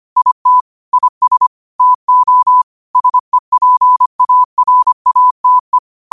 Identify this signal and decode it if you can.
Morse code, heard as a keyed tone, about 25 wpm, 1000 Hz. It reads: IT IS TO SEPARATE
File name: qotdfiles / 25wpm